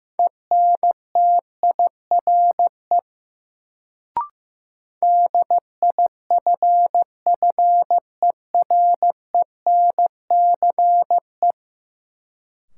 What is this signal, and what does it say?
Morse code, heard as a keyed tone, about 15 wpm, 700 Hz. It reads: ENTIRE DIFFERENCE
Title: Sentences from Top 800 Words - New Word Drill - 15wpm